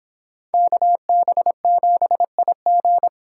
Morse code, heard as a keyed tone, about 30 wpm, 700 Hz. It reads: KB7IG